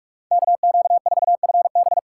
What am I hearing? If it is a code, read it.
Morse code, heard as a keyed tone, about 47 wpm, 700 Hz. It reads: KQ4FB
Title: USA Call Signs - 5 Characters - 6x Character Spacing - 45wpm